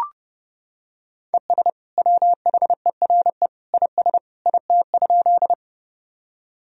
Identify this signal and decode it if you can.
Morse code, heard as a keyed tone, about 32 wpm, 700 Hz. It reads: ES WHERE IS IT?
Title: Sentences from Top 200 Words and Abbreviations 30wpm